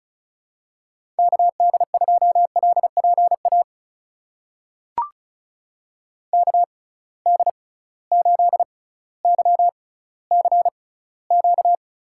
Morse code, heard as a keyed tone, about 36 wpm, spaced as for 18 wpm, 700 Hz. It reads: KD2LPA KD8YCQ
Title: USA Call Signs - 6 Characters - 6x Character Spacing - 35wpm